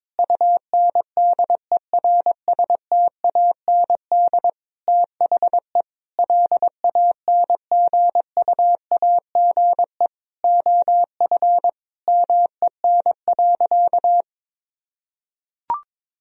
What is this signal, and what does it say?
Morse code, heard as a keyed tone, about 22 wpm, 700 Hz. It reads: UNDERSTAND THE LANGUAGE OF MEN.
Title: Sentences from Top 1000 Words - Review Part 3 - 22wpm